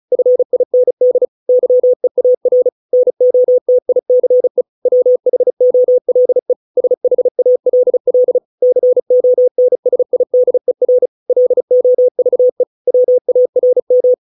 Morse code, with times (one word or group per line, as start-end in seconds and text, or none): 0.1-1.3 FIND
1.5-2.7 YEAR
2.9-4.6 NOTICE
4.9-6.5 WHOLE
6.8-8.4 SHALL
8.6-11.1 CONSIDER
11.3-12.6 LOVE
12.9-14.1 WARM